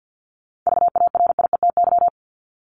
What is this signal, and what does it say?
Morse code, heard as a keyed tone, about 50 wpm, 700 Hz. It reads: VARIETY
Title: Top 501-750 Words 50wpm